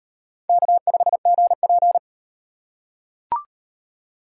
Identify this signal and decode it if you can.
Morse code, heard as a keyed tone, about 38 wpm, 700 Hz. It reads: K5GP